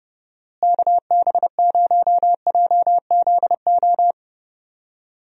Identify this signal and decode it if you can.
Morse code, heard as a keyed tone, about 30 wpm, 700 Hz. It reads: KB0JZO
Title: USA Call Signs - 6 Characters - Repeated 2x - 30wpm